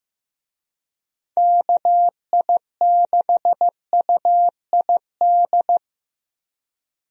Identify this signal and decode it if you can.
Morse code, heard as a keyed tone, about 15 wpm, 700 Hz. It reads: KI6UID
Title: USA Call Signs - 6 Characters - Repeated 2x - 15wpm